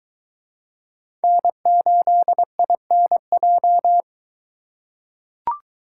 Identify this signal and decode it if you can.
Morse code, heard as a keyed tone, about 22 wpm, 700 Hz. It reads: N8INJ